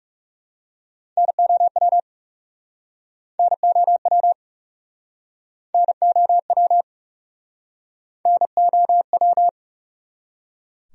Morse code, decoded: NOW NOW NOW NOW

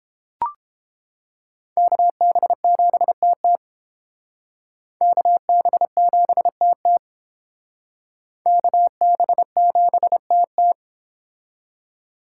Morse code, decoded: KB7TT KB7TT KB7TT

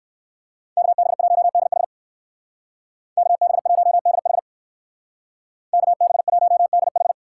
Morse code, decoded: KB1DH KB1DH KB1DH

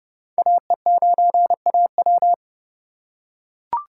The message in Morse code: AE9AW

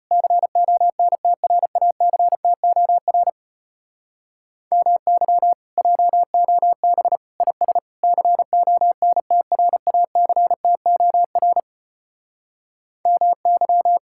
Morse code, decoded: CONTRACTOR MY JOB IS CONTRACTOR MY

